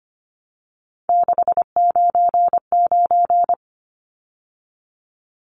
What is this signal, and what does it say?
Morse code, heard as a keyed tone, about 25 wpm, 700 Hz. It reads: 699
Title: Numbers - 3-Digit 25wpm